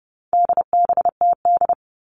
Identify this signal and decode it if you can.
Morse code, heard as a keyed tone, about 30 wpm, 700 Hz. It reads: DBTD
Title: All Cut Numbers - 4-Digit 30wpm